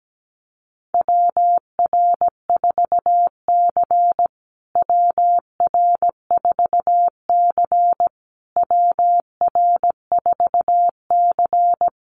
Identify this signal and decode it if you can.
Morse code, heard as a keyed tone, about 17 wpm, 700 Hz. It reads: WR4C WR4C WR4C
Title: USA Call Signs Repeated 3x 17wpm